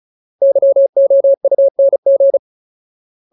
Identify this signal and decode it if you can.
Morse code, heard as a keyed tone, about 34 wpm, 550 Hz. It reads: YOUNG
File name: Ultimate Word List / 35WPM Ultimate